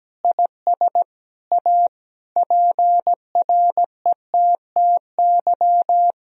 Morse code, heard as a keyed tone, about 17 wpm, 700 Hz. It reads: IS A PRETTY
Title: Sentences from Top 1000 Words - Review Part 2 - 17wpm